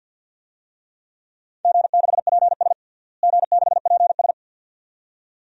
Morse code, heard as a keyed tone, about 50 wpm, 700 Hz. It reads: G6PS G6PS
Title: DXCC-100 Call Signs - 4 Characters - Repeated 2x - 50wpm